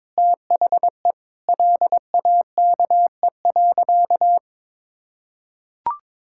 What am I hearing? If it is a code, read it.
Morse code, heard as a keyed tone, about 22 wpm, 700 Hz. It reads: THE LAKE.
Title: Sentences from Top 1600 Words 22wpm